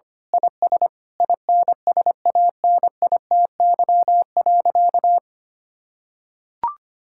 Morse code, heard as a keyed tone, about 25 wpm, 700 Hz. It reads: IS INSANITY.
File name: practice / 25wpm / Sentences with 9-Word Phrases - 25wpm